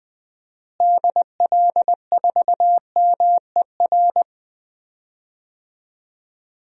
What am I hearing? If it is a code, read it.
Morse code, heard as a keyed tone, about 20 wpm, 700 Hz. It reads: DL4MER
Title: DXCC-100 Call Signs - 6 Characters - Repeated 2x - 20wpm